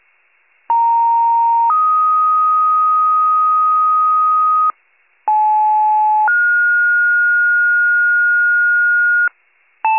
Two-tone page, heard as a one-second tone, then a three-second tone, about 930 then 1250 Hz, about 860 then 1430 Hz, with noise. It is heard from 0.7 to 9.3 s.
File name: wcalert